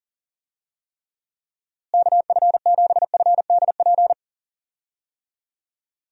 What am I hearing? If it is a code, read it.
Morse code, heard as a keyed tone, about 37 wpm, 700 Hz. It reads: KF7FDP
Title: USA Call Signs - 6 Characters - Repeated 3x - 40wpm